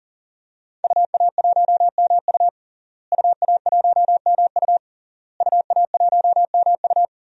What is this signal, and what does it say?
Morse code, heard as a keyed tone, about 38 wpm, 700 Hz. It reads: UA1MU UA1MU UA1MU
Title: DXCC-100 Call Signs - 5 Characters - Repeated 3x - 40wpm